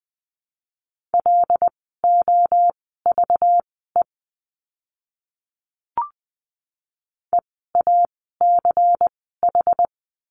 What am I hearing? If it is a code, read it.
Morse code, heard as a keyed tone, about 20 wpm, spaced as for 15 wpm, 700 Hz. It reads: LOVE EACH
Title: 192 Lesson 31f - Letter v - Words 20wpm